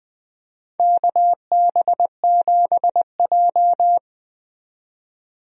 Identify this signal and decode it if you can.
Morse code, heard as a keyed tone, about 20 wpm, 700 Hz. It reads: KB7J
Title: USA Call Signs - 4 Characters - 20wpm